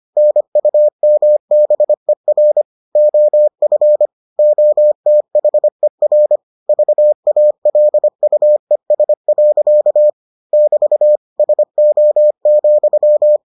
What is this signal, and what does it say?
Morse code, heard as a keyed tone, about 25 wpm, 600 Hz. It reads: NUMBER OF OTHER VALUES. <BT> SO,